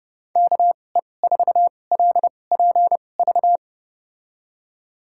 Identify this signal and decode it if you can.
Morse code, heard as a keyed tone, about 29 wpm, spaced as for 21 wpm, 700 Hz. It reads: KE4LPV
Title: USA Call Signs - 6 Characters - 2x Character Spacing - 30wpm